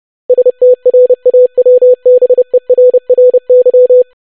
Morse code, heard as a keyed tone, about 30 wpm, 500 Hz. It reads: STRAWBERRY
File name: longwords-25wpm